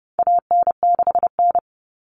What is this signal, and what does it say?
Morse code, heard as a keyed tone, about 30 wpm, 700 Hz. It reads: AN6N
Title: Common Cut Numbers - 4-Digit 30wpm